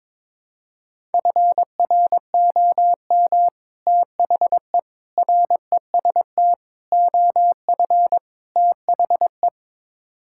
Morse code, heard as a keyed tone, about 22 wpm, 700 Hz. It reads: FROM THE REST OF THE
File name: practice / 22wpm / Build A Phrase - 5-Word Phrases - 22wpm